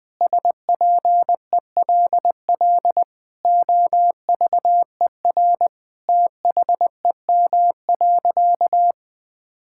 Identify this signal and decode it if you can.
Morse code, heard as a keyed tone, about 21 wpm, 700 Hz. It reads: SPELL OVER THEM.
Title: Sentences from Top 1000 Words - Review Part 6 - 20wpm